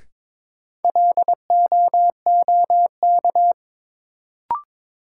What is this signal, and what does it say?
Morse code, heard as a keyed tone, about 22 wpm, 700 Hz. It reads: LOOK
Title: Sentences from Top 100 Words - New Word Drill - 22wpm